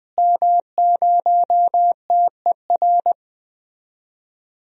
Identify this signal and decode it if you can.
Morse code, heard as a keyed tone, about 20 wpm, 700 Hz. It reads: M0TER
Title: DXCC-100 Call Signs - 5 Characters - Repeated 2x - 20wpm